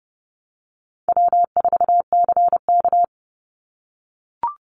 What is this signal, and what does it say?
Morse code, heard as a keyed tone, about 29 wpm, 700 Hz. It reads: W4CK
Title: Easy Call Signs - 30wpm